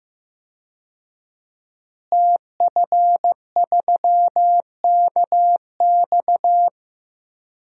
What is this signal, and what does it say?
Morse code, heard as a keyed tone, about 15 wpm, 700 Hz. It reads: TF3KX
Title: DXCC-100 Call Signs - 5 Characters - Repeated 2x - 15wpm